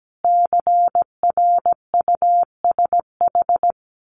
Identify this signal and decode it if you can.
Morse code, heard as a keyed tone, about 17 wpm, 700 Hz. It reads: CRUSH
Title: A Sound Of Thunder 17wpm